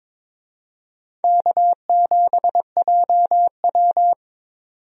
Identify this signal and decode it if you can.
Morse code, heard as a keyed tone, about 22 wpm, 700 Hz. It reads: K7JW
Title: USA Call Signs - 4 Characters - 22wpm